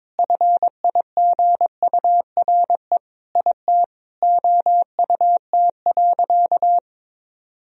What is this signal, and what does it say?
Morse code, heard as a keyed tone, about 22 wpm, 700 Hz. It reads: FIGURE IT OUT.
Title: Sentences from Top 1000 Words - Review Part 2 - 22wpm